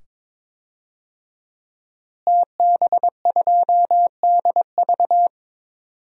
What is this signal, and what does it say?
Morse code, heard as a keyed tone, about 22 wpm, 700 Hz. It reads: TB2DV
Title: DXCC-100 Call Signs - 5 Characters - Repeated 2x - 22wpm